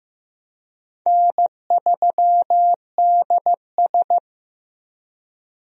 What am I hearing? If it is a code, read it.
Morse code, heard as a keyed tone, about 15 wpm, 700 Hz. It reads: N3DS
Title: USA Call Signs - 4 Characters - Repeated 2x - 15wpm